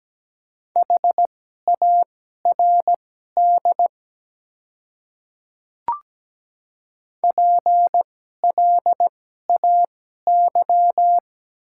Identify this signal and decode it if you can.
Morse code, heard as a keyed tone, about 17 wpm, spaced as for 12 wpm, 690 Hz. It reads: HARD PLAY
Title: 2x Character Spacing - Top 300 Words - 17wpm